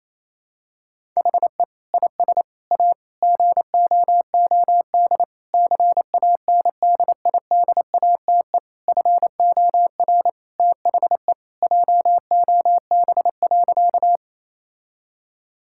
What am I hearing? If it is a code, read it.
Morse code, heard as a keyed tone, about 28 wpm, 700 Hz. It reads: HE IS A GOOD CANDIDATE FOR THE JOB.
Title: Sentences from Top 1600 Words 28wpm